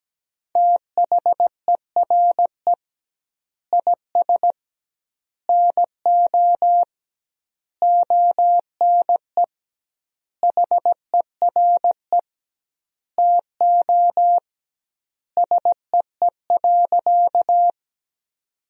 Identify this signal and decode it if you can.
Morse code, heard as a keyed tone, about 17 wpm, 700 Hz. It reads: THERE IS NO ONE HERE TO SEE.
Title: 2x Word-Spacing - Sentences from Top 200 Words - 17wpm